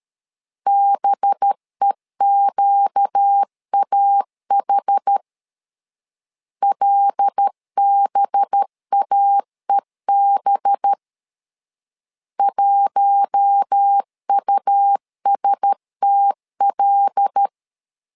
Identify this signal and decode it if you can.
Morse code, heard as a keyed tone, about 13 wpm, 800 Hz. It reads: BEQAH LBAEB 1USTL
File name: LEZIONI / 5 LBVC / 5 LB 12wpm 1906